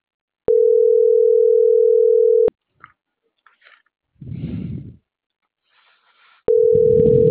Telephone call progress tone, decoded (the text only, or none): ringback tone